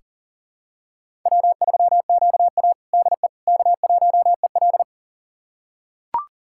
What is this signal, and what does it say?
Morse code, heard as a keyed tone, about 40 wpm, 700 Hz. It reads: W3QA DE K1EL
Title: USA Callsign DE Callsign - 4 Characters - 40wpm